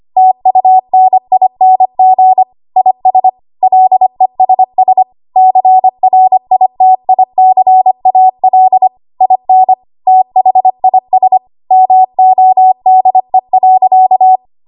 Morse code, heard as a keyed tone, about 25 wpm, 750 Hz. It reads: TUNING IS LESS CRITICAL IN THIS MODE.
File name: Archive / 25 WPM / 25WPM